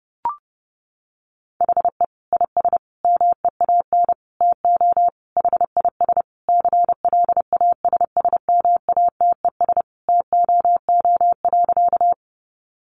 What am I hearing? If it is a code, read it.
Morse code, heard as a keyed tone, about 30 wpm, 700 Hz. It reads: HE IS MEAN TO HIS CLASSMATES TOO.
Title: Additional Easy Phrases 30wpm